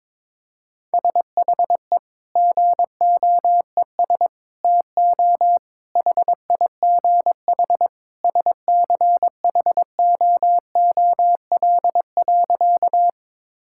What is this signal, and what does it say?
Morse code, heard as a keyed tone, about 21 wpm, 700 Hz. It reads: SHE GOES TO HIGH SCHOOL.
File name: practice / 22wpm / Sentences with 2-Word Phrases - 22wpm